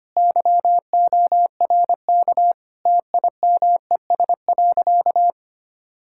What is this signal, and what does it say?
Morse code, heard as a keyed tone, about 25 wpm, 700 Hz. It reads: YORK TIMES.
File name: practice / 25wpm / Sentences with 4-Word Phrases - 25wpm